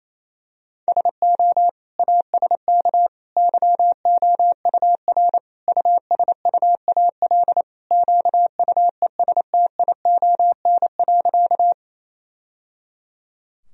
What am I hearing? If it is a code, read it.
Morse code, heard as a keyed tone, about 28 wpm, 700 Hz. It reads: SO ASK YOUR USUAL QUESTION.
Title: Sentences from Top 1000 Words - Review Part 2 - 28wpm